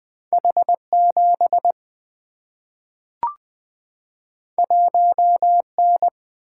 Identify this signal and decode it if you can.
Morse code, heard as a keyed tone, about 20 wpm, 700 Hz. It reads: H 7 1 N
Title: Single-Letter Single-Number 20wpm